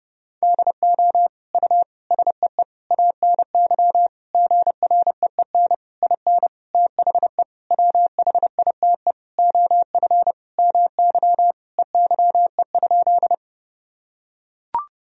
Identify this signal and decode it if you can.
Morse code, heard as a keyed tone, about 32 wpm, 700 Hz. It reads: DO U SEE ANY GREEN IN THE WHITE OF MY EYE?
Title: Sentences from Top 500 Words and Abbreviations 30wpm